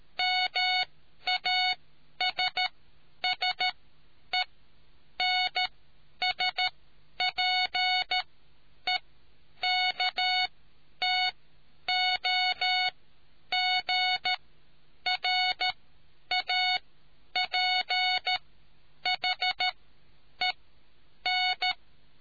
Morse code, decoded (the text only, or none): MASSENSPEKTOGRAPHEN